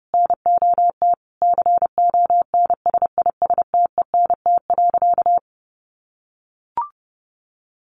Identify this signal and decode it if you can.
Morse code, heard as a keyed tone, about 30 wpm, 700 Hz. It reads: NOT CONSISTENT.
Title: Sentences from Top 1500 Words 30wpm